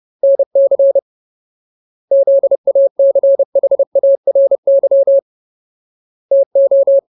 Morse code, heard as a keyed tone, about 30 wpm, 550 Hz. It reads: NC ZACHARY TO
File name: Ultimate Word List / 30WPM Ultimate